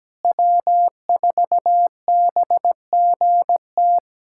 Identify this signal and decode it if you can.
Morse code, heard as a keyed tone, about 17 wpm, 700 Hz. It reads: W4BGT
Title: USA Callsign DE Callsign - 5 Characters - 17wpm